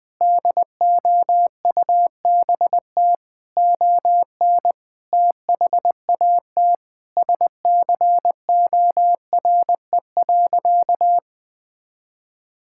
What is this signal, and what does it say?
Morse code, heard as a keyed tone, about 20 wpm, 700 Hz. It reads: DOUBT ON THAT SCORE.
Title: Sentences from Top 1600 Words 20wpm